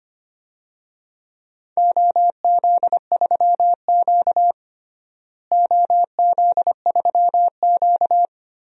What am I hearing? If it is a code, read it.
Morse code, heard as a keyed tone, about 25 wpm, 700 Hz. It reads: OZ3Q OZ3Q
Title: DXCC-100 Call Signs - 4 Characters - Repeated 3x - 25wpm